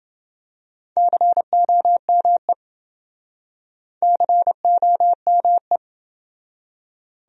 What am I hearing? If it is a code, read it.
Morse code, heard as a keyed tone, about 28 wpm, 700 Hz. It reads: COME COME